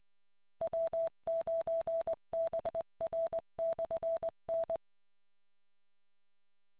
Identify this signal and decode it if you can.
Morse code, heard as a keyed tone, about 20 wpm, 660 Hz. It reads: W9BR/N